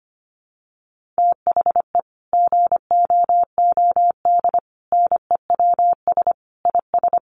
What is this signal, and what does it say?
Morse code, heard as a keyed tone, about 25 wpm, 700 Hz. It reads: THE GOOD NEWS IS
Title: Sets of 5 Words 25wpm